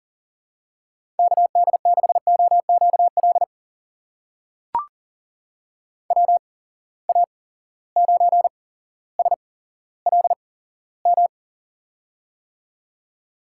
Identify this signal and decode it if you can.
Morse code, heard as a keyed tone, about 39 wpm, spaced as for 17 wpm, 700 Hz. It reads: KD6OQL WA9SLM
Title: USA Call Signs - 6 Characters - 8x Character Spacing - 40wpm